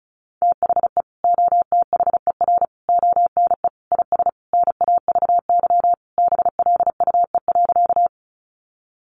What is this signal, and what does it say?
Morse code, heard as a keyed tone, about 34 wpm, 700 Hz. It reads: THE OTHER ONE IS NAVY BLUE.
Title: Additional Easy Phrases 35wpm